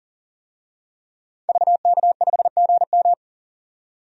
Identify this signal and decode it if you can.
Morse code, heard as a keyed tone, about 40 wpm, 700 Hz. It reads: VK5GM
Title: DXCC-100 Call Signs - 5 Characters - 40wpm